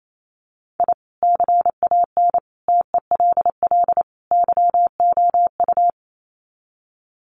Morse code, heard as a keyed tone, about 28 wpm, 700 Hz. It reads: I CAN TELL YOU
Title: Sets of 4 Words 28wpm